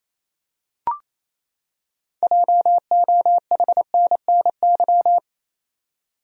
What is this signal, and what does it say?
Morse code, heard as a keyed tone, about 28 wpm, 700 Hz. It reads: JOHNNY
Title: USA Common Names - 28wpm